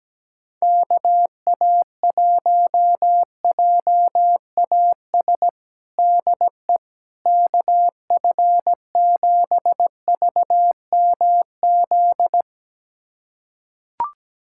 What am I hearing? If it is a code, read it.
Morse code, heard as a keyed tone, about 17 wpm, 700 Hz. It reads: KA1JAS DE KF7VMZ
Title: USA Callsign DE Callsign - 6 Characters - 17wpm